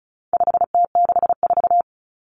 Morse code, heard as a keyed tone, about 34 wpm, 700 Hz. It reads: 5T64
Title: Common Cut Numbers - 4-Digit 35wpm